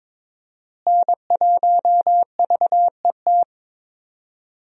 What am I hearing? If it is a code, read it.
Morse code, heard as a keyed tone, about 22 wpm, 700 Hz. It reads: N1VET
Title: USA Call Signs - 5 Characters - Repeated 3x - 22wpm